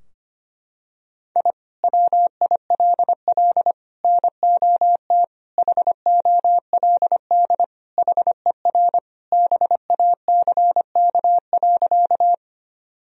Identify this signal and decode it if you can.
Morse code, heard as a keyed tone, about 25 wpm, 700 Hz. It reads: I WILL NOT HOLD HER BACK.
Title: Sentences from Top 1000 Words - Review Part 2 - 25wpm